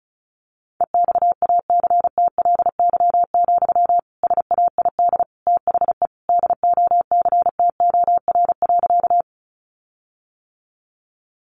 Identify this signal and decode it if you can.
Morse code, heard as a keyed tone, about 34 wpm, 700 Hz. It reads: EXACTLY, SAID THE DOCTOR.
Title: Sentences from Top 1500 Words 35wpm